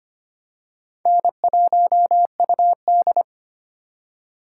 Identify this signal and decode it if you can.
Morse code, heard as a keyed tone, about 25 wpm, 700 Hz. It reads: N1UD
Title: USA Call Signs - 4 Characters - 3x Character Spacing - 25wpm